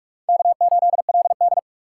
Morse code, heard as a keyed tone, about 58 wpm, 700 Hz. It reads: K8LD